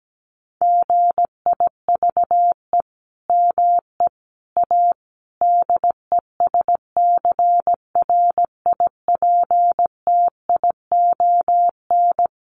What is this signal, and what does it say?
Morse code, heard as a keyed tone, about 17 wpm, 700 Hz. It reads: GIVE ME A DESCRIPTION
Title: Sentences from Top 1300 Words 17wpm